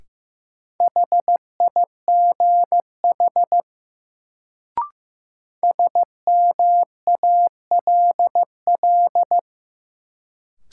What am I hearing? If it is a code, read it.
Morse code, heard as a keyed tone, about 15 wpm, 700 Hz. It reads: HIGH SMALL
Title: Sentences from Top 200 Words - New Word Drill - 15wpm